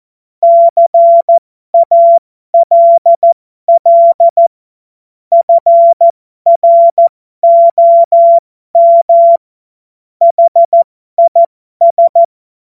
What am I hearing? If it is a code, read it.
Morse code, heard as a keyed tone, about 14 wpm, spaced as for 12 wpm, 680 Hz. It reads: CALL FROM HIS
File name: Morsekurs / Witze